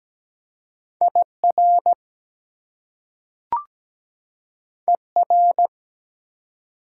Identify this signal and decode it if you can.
Morse code, heard as a keyed tone, about 17 wpm, 700 Hz. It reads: IR ER
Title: Prefixes and Suffixes 17wpm